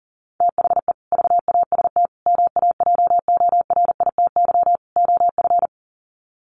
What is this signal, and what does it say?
Morse code, heard as a keyed tone, about 40 wpm, 700 Hz. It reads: THE VAST MAJORITY OF
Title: Sets of 4 Words 40wpm